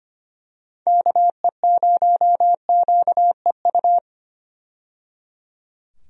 Morse code, heard as a keyed tone, about 25 wpm, 700 Hz. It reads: KE0QEU
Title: USA Call Signs - 6 Characters - 25wpm